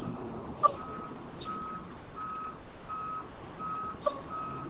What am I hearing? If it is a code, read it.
There is no code.